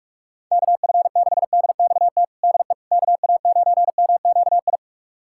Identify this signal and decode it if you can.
Morse code, heard as a keyed tone, about 47 wpm, 700 Hz. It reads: KF6DXT DE KA9MQI
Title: USA Callsign DE Callsign - 6 Characters - 45wpm